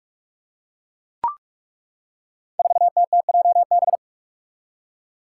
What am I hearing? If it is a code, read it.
Morse code, heard as a keyed tone, about 41 wpm, 700 Hz. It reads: 4TTJB
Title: Five Challenging Letter-Number Combinations 45wpm